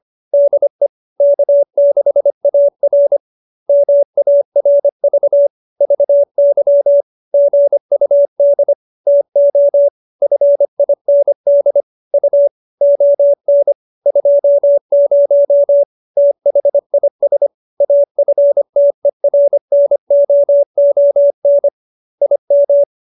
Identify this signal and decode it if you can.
Morse code, heard as a keyed tone, about 25 wpm, 570 Hz. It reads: DE K6AR MARV VY GUD TO FIND U ON 20 THIS AFTERNOON IM
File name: QSO312 25